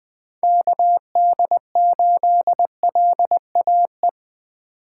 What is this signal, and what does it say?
Morse code, heard as a keyed tone, about 21 wpm, 700 Hz. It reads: KD8LAE
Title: Speed Racing - USA 2x3 Call Signs 17wpm